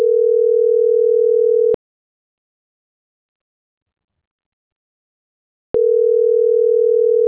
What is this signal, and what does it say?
Ringback tone (North America), 440 + 480 Hz.